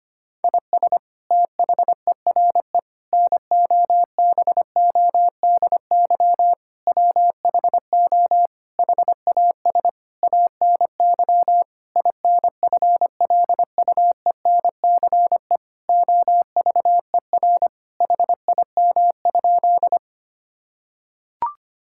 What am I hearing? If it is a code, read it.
Morse code, heard as a keyed tone, about 25 wpm, 700 Hz. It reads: IS THERE NOBODY WHO HAS ANY INFLUENCE OVER HIM?
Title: Sentences from Top 1600 Words 25wpm